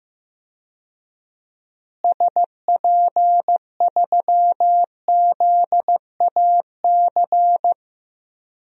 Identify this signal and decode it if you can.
Morse code, heard as a keyed tone, about 15 wpm, 700 Hz. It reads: SP3ZAC
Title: DXCC-100 Call Signs - 6 Characters - Repeated 3x - 15wpm